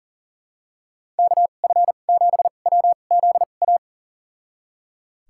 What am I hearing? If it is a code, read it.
Morse code, heard as a keyed tone, about 40 wpm, spaced as for 29 wpm, 700 Hz. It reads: KF7WZA
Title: USA Call Signs - 6 Characters - 2x Character Spacing - 40wpm